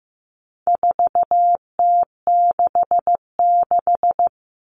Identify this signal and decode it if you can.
Morse code, heard as a keyed tone, about 15 wpm, 700 Hz. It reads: 4T66
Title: All Cut Numbers - 4-Digit 15wpm